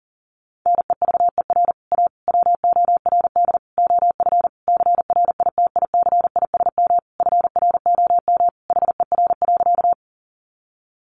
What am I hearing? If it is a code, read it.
Morse code, heard as a keyed tone, about 40 wpm, 700 Hz. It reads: NEVER A WORD OF CRITICISM FROM HER.
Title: Sentences from Top 1300 Words 40wpm